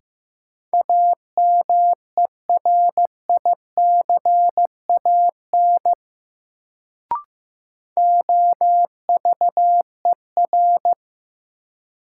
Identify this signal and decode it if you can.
Morse code, heard as a keyed tone, about 15 wpm, 700 Hz. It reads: AMERICAN OVER
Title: Sets of 2 Words - New Word Drill - 15wpm